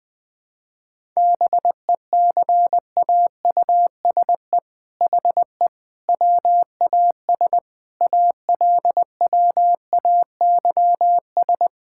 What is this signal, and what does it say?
Morse code, heard as a keyed tone, about 20 wpm, 700 Hz. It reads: BECAUSE HE WAS ALWAYS
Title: All-Age Appropriate Jokes - 20wpm